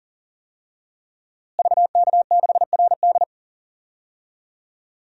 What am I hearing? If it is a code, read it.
Morse code, heard as a keyed tone, about 40 wpm, 700 Hz. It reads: VK6RD
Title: DXCC-100 Call Signs - 5 Characters - Repeated 3x - 40wpm